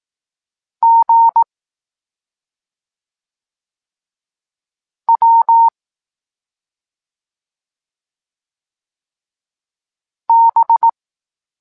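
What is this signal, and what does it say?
Morse code, heard as a keyed tone, about 18 wpm, 920 Hz. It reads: G W B
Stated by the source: Evenly spaced Morse lesson text 21 for learning letters G and Z, from SP9MOA Niepołomicki Klub Krótkofalowców.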